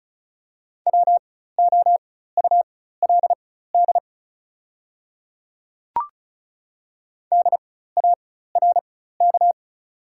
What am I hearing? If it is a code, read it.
Morse code, heard as a keyed tone, about 34 wpm, spaced as for 16 wpm, 690 Hz. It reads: WOULD DARK
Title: 4x Character Spacing - Top 500 Words - 35wpm